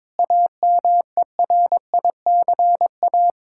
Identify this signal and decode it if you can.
Morse code, heard as a keyed tone, about 22 wpm, 700 Hz. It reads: AMERICA
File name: practice / 22wpm / Build A Phrase - 5-Word Phrases - 22wpm